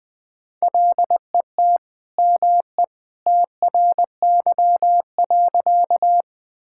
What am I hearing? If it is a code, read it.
Morse code, heard as a keyed tone, about 20 wpm, 700 Hz. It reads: LET ME TRY.
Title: Sentences with 2-Word Phrases - 20wpm